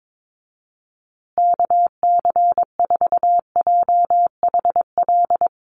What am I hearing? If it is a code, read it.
Morse code, heard as a keyed tone, about 22 wpm, 700 Hz. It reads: KC4JHL
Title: USA Call Signs Repeated 2x 22wpm